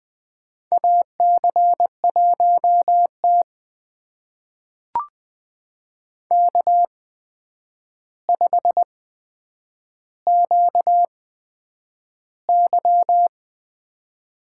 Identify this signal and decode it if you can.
Morse code, heard as a keyed tone, about 20 wpm, spaced as for 9 wpm, 700 Hz. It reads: AC1T K5QY
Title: USA Call Signs - 4 Characters - 8x Character Spacing - 20wpm